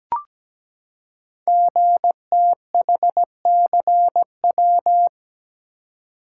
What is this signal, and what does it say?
Morse code, heard as a keyed tone, about 17 wpm, 700 Hz. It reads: GTHCW
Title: Code Groups - 5 Letters - 17wpm